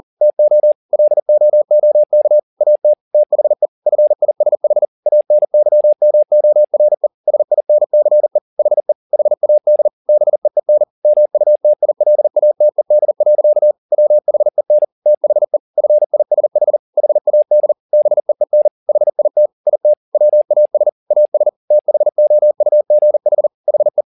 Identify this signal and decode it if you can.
Morse code, heard as a keyed tone, about 40 wpm, 600 Hz. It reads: TO LOOK AT THE FISH ANYMORE SINCE HE HAD BEEN MUTILATED. WHEN THE FISH HAD BEEN HIT IT WAS AS THOUGH HE
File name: the old man and the sea 2 40wpm